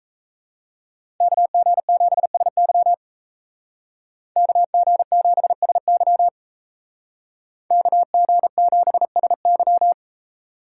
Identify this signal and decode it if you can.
Morse code, heard as a keyed tone, about 33 wpm, 700 Hz. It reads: KG7SY KG7SY KG7SY
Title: Speed Racing - USA 2x2 Call Signs 28wpm